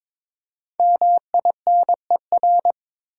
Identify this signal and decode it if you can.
Morse code, heard as a keyed tone, about 22 wpm, 700 Hz. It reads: MINER
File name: 5 Letter Words - First 3 Letters Given - 22wpm